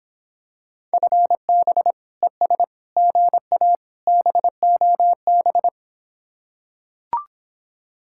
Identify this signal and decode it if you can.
Morse code, heard as a keyed tone, about 26 wpm, 700 Hz. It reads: FB ES GA BOB